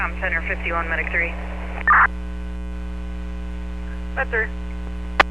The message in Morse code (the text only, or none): none